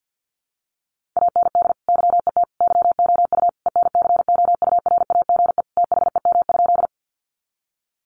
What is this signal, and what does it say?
Morse code, heard as a keyed tone, about 50 wpm, 700 Hz. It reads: AND YET YOU ENCOURAGE THEM?
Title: Sentences from Top 1400 Words 50wpm